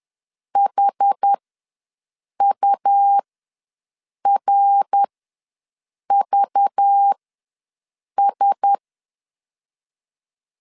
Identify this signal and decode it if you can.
Morse code, heard as a keyed tone, about 10 wpm, spaced as for 6 wpm, 800 Hz. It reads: HURVS
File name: LEZIONI / 6 PKX0 / 6 PKX ws 10wpm 3706